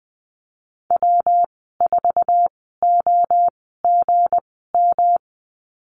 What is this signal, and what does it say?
Morse code, heard as a keyed tone, about 20 wpm, spaced as for 15 wpm, 700 Hz. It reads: W4OGM
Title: 173 Lesson 28g - Number 9 - Call Signs 20wpm